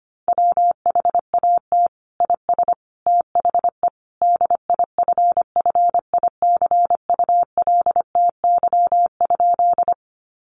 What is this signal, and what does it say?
Morse code, heard as a keyed tone, about 25 wpm, 700 Hz. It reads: WHAT IS THE DIFFICULTY?
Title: Sentences from Top 1500 Words 25wpm